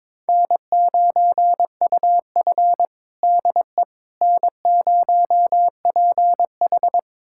Morse code, heard as a keyed tone, about 22 wpm, 700 Hz. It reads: N9UF DE N0PH
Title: USA Callsign DE Callsign - 4 Characters - 22wpm